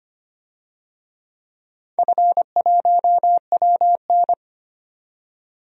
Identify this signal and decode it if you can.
Morse code, heard as a keyed tone, about 25 wpm, 700 Hz. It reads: F1WN
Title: DXCC-100 Call Signs - 4 Characters - 25wpm